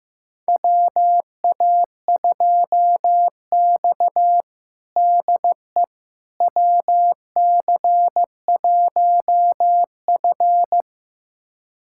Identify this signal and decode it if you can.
Morse code, heard as a keyed tone, about 15 wpm, 700 Hz. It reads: WA2X DE WC1F
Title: USA Callsign DE Callsign - 4 Characters - 15wpm